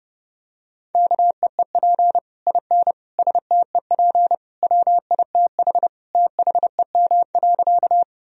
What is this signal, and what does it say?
Morse code, heard as a keyed tone, about 30 wpm, 700 Hz. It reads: KEEP IN STEP WITH THEM.
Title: Sentences from Top 1000 Words - Review Part 2 - 30wpm